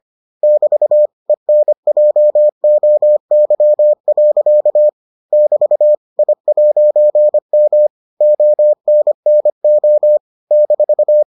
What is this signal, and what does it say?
Morse code, heard as a keyed tone, about 25 wpm, 600 Hz. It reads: <BT> ENJOY. <BT> I'M ONNO -